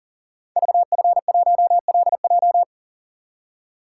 Morse code, heard as a keyed tone, about 40 wpm, 700 Hz. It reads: VF1LJ
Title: Code Groups - 5 Letter:Numbers - 40wpm